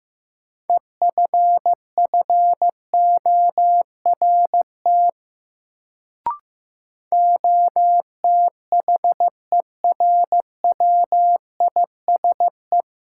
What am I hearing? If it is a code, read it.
Morse code, heard as a keyed tone, about 16 wpm, 700 Hz. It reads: EFFORT OTHERWISE
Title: Sentences from Top 1200 Words - New Word Drill - 15wpm